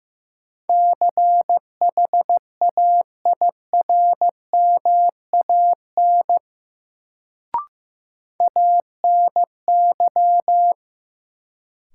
Morse code, heard as a keyed tone, about 15 wpm, 700 Hz. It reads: CHAIRMAN ANY
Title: Sets of 3 Words - New Word Drill - 15wpm